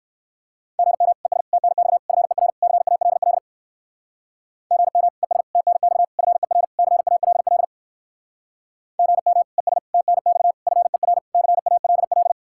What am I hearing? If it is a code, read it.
Morse code, heard as a keyed tone, about 63 wpm, 700 Hz. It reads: GM ES TNX FER CALL GM ES TNX FER CALL GM ES TNX FER CALL